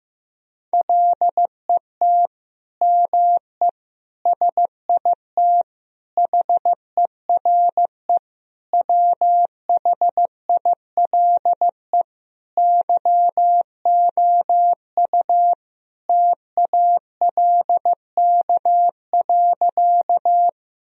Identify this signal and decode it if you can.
Morse code, heard as a keyed tone, about 15 wpm, 700 Hz. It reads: LET ME SIT HERE WHILE YOU TALK.
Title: Sentences from Top 1000 Words - Review Part 1 - 15wpm